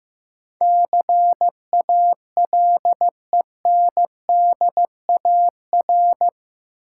Morse code, heard as a keyed tone, about 15 wpm, 700 Hz. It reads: CALENDAR
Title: Sentences from Top 1600 Words - New Word Drill Alt - 15wpm